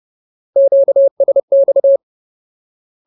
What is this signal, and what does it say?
Morse code, heard as a keyed tone, about 30 wpm, 550 Hz. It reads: QSX
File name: Ultimate Word List / 30WPM Ultimate